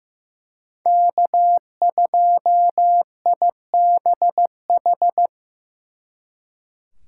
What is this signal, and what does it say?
Morse code, heard as a keyed tone, about 18 wpm, 700 Hz. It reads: K2IBH
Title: Speed Racing - USA 1x3 Call Signs 15wpm